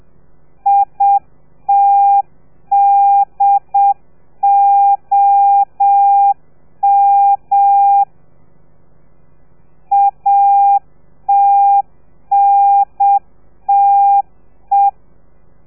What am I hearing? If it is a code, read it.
Morse code, heard as a keyed tone, about 7 wpm, 800 Hz. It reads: ITDOM ATNTE